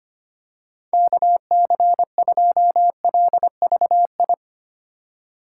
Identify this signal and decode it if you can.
Morse code, heard as a keyed tone, about 25 wpm, 700 Hz. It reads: KC2LVI
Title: USA Call Signs - 6 Characters - 3x Character Spacing - 25wpm